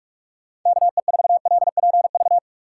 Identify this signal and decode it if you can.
Morse code, heard as a keyed tone, about 46 wpm, 700 Hz. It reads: KE4LPV
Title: USA Call Signs - 6 Characters - 3x Character Spacing - 45wpm